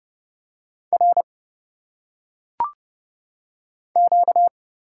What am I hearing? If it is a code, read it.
Morse code, heard as a keyed tone, about 29 wpm, 700 Hz. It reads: R Q
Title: Single Letters - Phonetic Pronunciation 30wpm